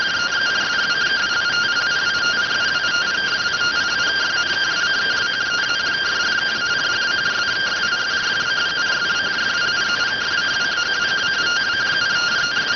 Two-tone teleprinter signal, about 170 Hz shift.